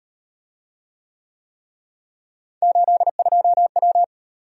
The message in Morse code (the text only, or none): 82W